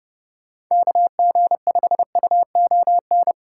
Morse code, heard as a keyed tone, about 40 wpm, spaced as for 35 wpm, 700 Hz. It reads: KG5UON
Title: Speed-Racing Common QSO Phrases - 30wpm